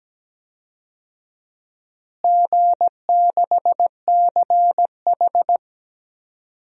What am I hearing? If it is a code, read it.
Morse code, heard as a keyed tone, about 17 wpm, 700 Hz. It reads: G6CH
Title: DXCC-100 Call Signs - 4 Characters - Repeated 2x - 17wpm